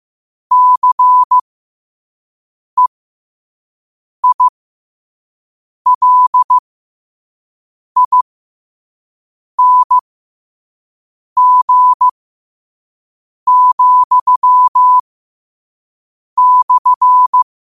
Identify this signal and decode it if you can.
Morse code, heard as a keyed tone, about 15 wpm, spaced as for 5 wpm, 1000 Hz. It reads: CEILING,/
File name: qotdfiles / 5wpm